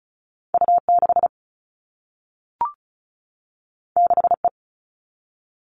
Morse code, heard as a keyed tone, about 36 wpm, 700 Hz. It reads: U6 6E
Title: All Cut Numbers - 2-Digit 35wpm